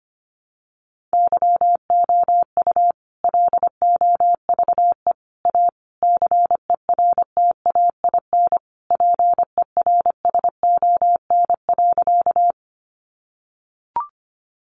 Morse code, heard as a keyed tone, about 25 wpm, 700 Hz. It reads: YOU LOVE A CERTAIN PERSON.
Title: Sentences from Top 500 Words 25wpm